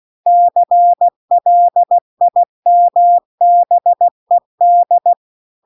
Morse code, heard as a keyed tone, about 16 wpm, 700 Hz. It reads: CLIMBED